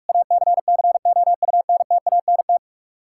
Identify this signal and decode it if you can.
Morse code, heard as a keyed tone, about 47 wpm, 700 Hz. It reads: ACCOUNTANT
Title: Practice QSO - Exchange 3 - More Info - 45wpm